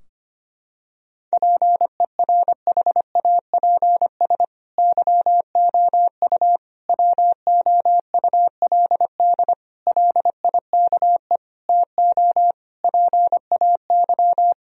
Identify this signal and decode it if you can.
Morse code, heard as a keyed tone, about 25 wpm, 700 Hz. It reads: PERHAPS YOU WOULD LIKE TO PAY
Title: Sentences from Top 1000 Words - Review Part 2 - 25wpm